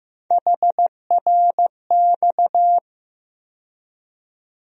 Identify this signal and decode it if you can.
Morse code, heard as a keyed tone, about 15 wpm, 700 Hz. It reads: HRX
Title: USA Call Sign Postfixes - 3 Letters 15wpm